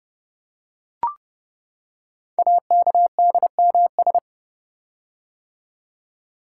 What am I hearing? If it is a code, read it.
Morse code, heard as a keyed tone, about 30 wpm, 700 Hz. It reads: AKDMS
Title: Code Groups - 5 Letters - 30wpm